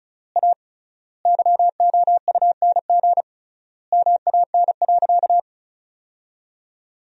Morse code, heard as a keyed tone, about 36 wpm, 700 Hz. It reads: A YOUNG MAN.
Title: 3x Word-Spacing - Sentences from Top 500 Words - 35wpm